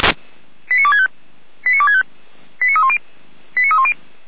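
Five-tone selective call (ZVEI), heard as quick runs of five tones, tones of about 70 ms. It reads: 89265 89265 89310 89310
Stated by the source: ZVEI five-tone call 89108 received on SDR, from the Feuerwehr Orlamünde (fire brigade) audio server.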